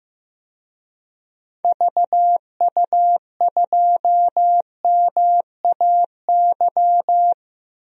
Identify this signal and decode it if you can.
Morse code, heard as a keyed tone, about 15 wpm, 700 Hz. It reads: VU2MAY
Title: DXCC-100 Call Signs - 6 Characters - Repeated 2x - 15wpm